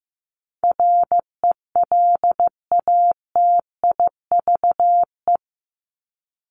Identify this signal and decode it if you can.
Morse code, heard as a keyed tone, about 15 wpm, 700 Hz. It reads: RELATIVE
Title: Top 1251-1500 Words 15wpm